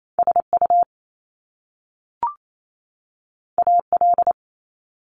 Morse code, heard as a keyed tone, about 28 wpm, 700 Hz. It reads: SU AL
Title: Top Two Letter Combinations in Words 28wpm